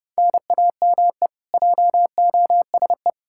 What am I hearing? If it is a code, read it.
Morse code, heard as a keyed tone, about 30 wpm, 700 Hz. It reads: NAME JOSE
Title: Practice QSO - Exchange 1 - Beginning - 30wpm